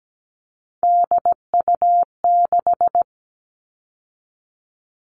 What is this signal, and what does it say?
Morse code, heard as a keyed tone, about 17 wpm, 700 Hz. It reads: DU6
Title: All Cut Numbers - 3-Digit 17wpm